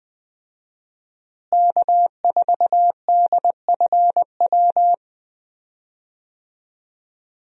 Morse code, heard as a keyed tone, about 21 wpm, 700 Hz. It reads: K4DFW
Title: USA Call Signs - 5 Characters - Repeated 2x - 20wpm